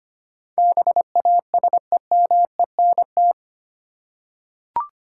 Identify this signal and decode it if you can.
Morse code, heard as a keyed tone, about 25 wpm, 700 Hz. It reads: BASEMENT
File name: Eight Letter Words - 25wpm